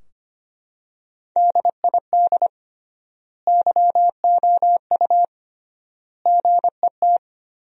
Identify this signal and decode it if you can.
Morse code, heard as a keyed tone, about 25 wpm, 700 Hz. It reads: DID YOU GET
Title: 3x Word-Spacing - Sentences from Top 400 Words - 25wpm